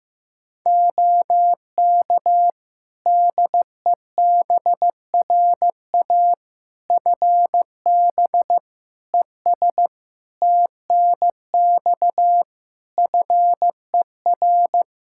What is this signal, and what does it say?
Morse code, heard as a keyed tone, about 15 wpm, 700 Hz. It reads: OK DEBRA FB ES TNX FER
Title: Practice QSO - Exchange 2 - Info - 15wpm